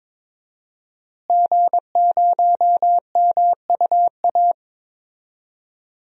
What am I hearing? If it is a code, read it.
Morse code, heard as a keyed tone, about 22 wpm, 700 Hz. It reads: G0MUA
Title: DXCC-100 Call Signs - 5 Characters - Repeated 3x - 22wpm